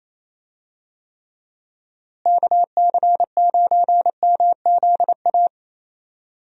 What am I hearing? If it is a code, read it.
Morse code, heard as a keyed tone, about 28 wpm, 700 Hz. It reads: KC9MZA
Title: USA Call Signs - 6 Characters - Repeated 3x - 28wpm